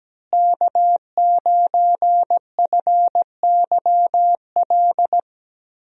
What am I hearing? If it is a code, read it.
Morse code, heard as a keyed tone, about 17 wpm, 700 Hz. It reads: K9FYL
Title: USA Call Signs - 5 Characters - Repeated 2x - 17wpm